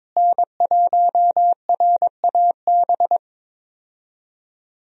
Morse code, heard as a keyed tone, about 22 wpm, 700 Hz. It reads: N1RAB
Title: USA Call Signs - 5 Characters - Repeated 2x - 22wpm